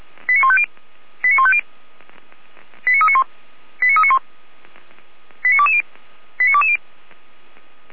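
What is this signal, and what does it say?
Five-tone selective call (ZVEI), heard as quick runs of five tones, tones of about 70 ms. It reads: 89150 89150 89391 89391 89220 89220
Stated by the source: ZVEI five-tone selective call received by SDR.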